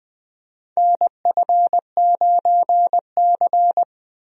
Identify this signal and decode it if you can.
Morse code, heard as a keyed tone, about 20 wpm, 700 Hz. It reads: NF9C
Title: USA Call Signs - 4 Characters - Repeated 3x - 20wpm